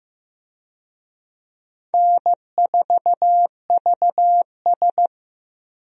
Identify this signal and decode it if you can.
Morse code, heard as a keyed tone, about 15 wpm, 700 Hz. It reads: N4VS
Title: USA Call Signs - 4 Characters - 15wpm